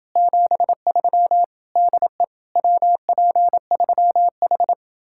Morse code, heard as a keyed tone, about 29 wpm, 700 Hz. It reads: 73 DE WP3H